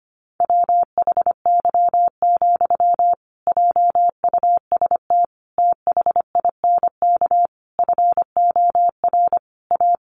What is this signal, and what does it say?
Morse code, heard as a keyed tone, about 25 wpm, 700 Hz. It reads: WHY, JUST THINK FOR A
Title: Sentences from Top 1000 Words 25wpm